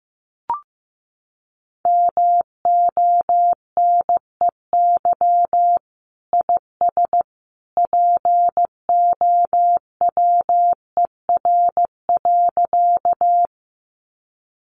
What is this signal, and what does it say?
Morse code, heard as a keyed tone, about 14 wpm, 700 Hz. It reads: MONEY IS POWER.
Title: Sentences from Top 500 Words 15wpm